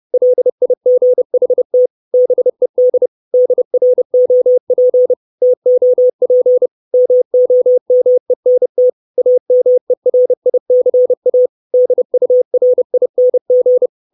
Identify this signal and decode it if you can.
Morse code, heard as a keyed tone, about 30 wpm, 500 Hz. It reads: LIGHT BED DROP TOP MOMENT AMERICA DURING
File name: 632 Words 30WPM